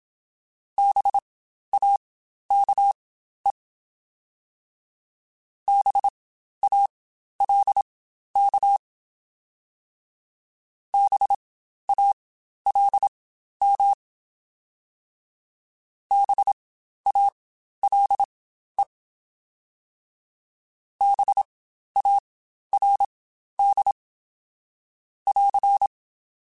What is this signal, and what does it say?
Morse code, heard as a keyed tone, about 27 wpm, spaced as for 12 wpm, 800 Hz. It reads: BAKE BALK BALM BALE BARD <AR>
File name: Spaced Alphas / Spaced Alphas mp3s / Bs sp @25